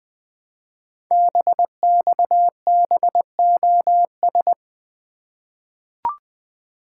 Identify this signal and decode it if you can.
Morse code, heard as a keyed tone, about 20 wpm, 700 Hz. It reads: BXBOS
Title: Code Groups - 5 Letters - 20wpm